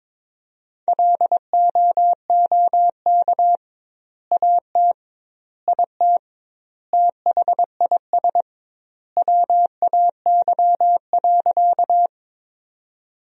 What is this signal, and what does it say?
Morse code, heard as a keyed tone, about 22 wpm, 700 Hz. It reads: LOOK AT IT THIS WAY.
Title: 2x Word-Spacing - Sentences from Top 100 Words - 22wpm